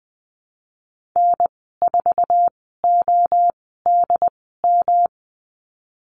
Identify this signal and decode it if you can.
Morse code, heard as a keyed tone, about 20 wpm, spaced as for 14 wpm, 700 Hz. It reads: N4ODM
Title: 128 Lesson 21g - Number 6 - Call Signs 20wpm